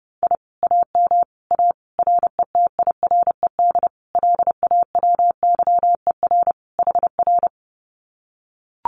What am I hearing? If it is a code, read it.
Morse code, heard as a keyed tone, about 28 wpm, 700 Hz. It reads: I AM A RETIRED LAWYER HR
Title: Common QSO Phrases Encore 30wpm